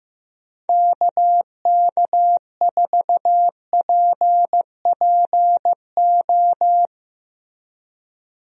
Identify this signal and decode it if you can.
Morse code, heard as a keyed tone, about 15 wpm, 700 Hz. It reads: KK4PPO
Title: USA Call Signs - 6 Characters - Repeated 2x - 15wpm